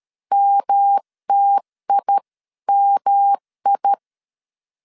Morse code, heard as a keyed tone, about 13 wpm, 800 Hz. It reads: MTIMI
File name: CW / DET / 1 3 1500C S A 12wpm 2905